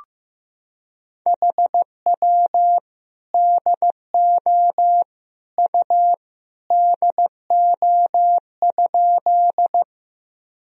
Morse code, heard as a keyed tone, about 15 wpm, 700 Hz. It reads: HW DO U DO?
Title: Sentences from Top 100 Words and Abbreviations 15wpm